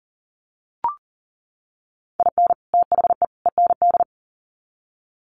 Morse code, heard as a keyed tone, about 40 wpm, 700 Hz. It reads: IN THE END
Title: Sets of 3 Words 40wpm